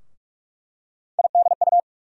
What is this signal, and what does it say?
Morse code, heard as a keyed tone, about 46 wpm, 700 Hz. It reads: IDU
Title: USA Call Sign Postfixes - 3 Letters 45wpm